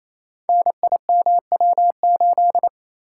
Morse code, heard as a keyed tone, about 27 wpm, 700 Hz. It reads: NIMW8
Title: Code Groups - 5 Letter:Numbers - 28wpm